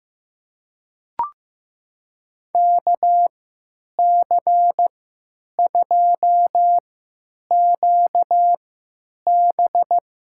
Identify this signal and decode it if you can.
Morse code, heard as a keyed tone, about 15 wpm, spaced as for 9 wpm, 700 Hz. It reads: KC2QB
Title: USA Call Signs - 3x Character Spacing - 15wpm